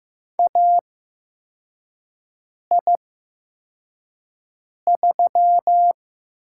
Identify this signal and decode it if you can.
Morse code, heard as a keyed tone, about 15 wpm, spaced as for 4 wpm, 700 Hz. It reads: AI3
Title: USA Call Signs - 4 Characters - 8x Character Spacing - 15wpm